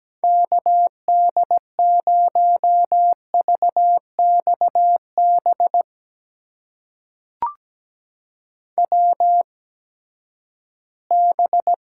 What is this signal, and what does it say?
Morse code, heard as a keyed tone, about 17 wpm, spaced as for 12 wpm, 700 Hz. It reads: KD0VXB WB
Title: USA Call Signs - 6 Characters - 8x Character Spacing - 17wpm